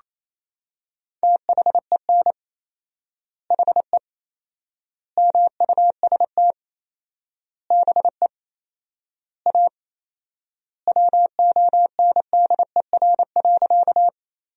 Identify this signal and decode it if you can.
Morse code, heard as a keyed tone, about 27 wpm, 700 Hz. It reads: THEN HE MUST BE A WONDER.
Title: 4x Word-Spacing - Sentences from Top 500 Words - 28wpm